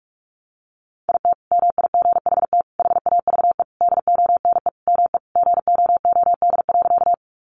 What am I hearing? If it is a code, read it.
Morse code, heard as a keyed tone, about 44 wpm, 700 Hz. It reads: IT MIGHT HAVE DONE ME GOOD.
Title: Sentences from Top 500 Words 45wpm